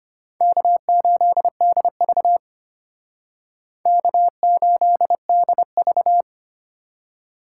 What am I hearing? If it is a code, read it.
Morse code, heard as a keyed tone, about 32 wpm, 700 Hz. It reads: K8DV K8DV